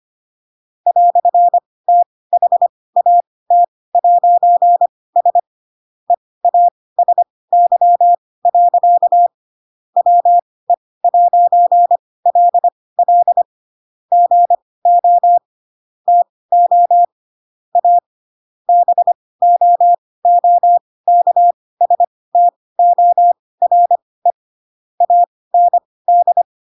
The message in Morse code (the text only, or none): "THAT'S EASY. WE'LL GO TO A BOOKSTORE AND